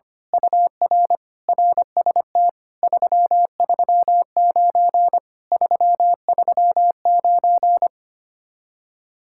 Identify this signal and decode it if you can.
Morse code, heard as a keyed tone, about 24 wpm, 700 Hz. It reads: UR RST 339 339
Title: Speed-Racing Common QSO Phrases - 25wpm